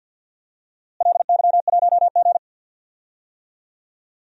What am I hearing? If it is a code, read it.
Morse code, heard as a keyed tone, about 48 wpm, 700 Hz. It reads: LY1G
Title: DXCC-100 Call Signs - 4 Characters - Repeated 3x - 50wpm